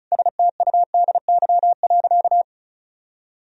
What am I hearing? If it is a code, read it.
Morse code, heard as a keyed tone, about 36 wpm, 700 Hz. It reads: STUDY.
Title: Sentences from Top 1000 Words - Review Part 1 - 35wpm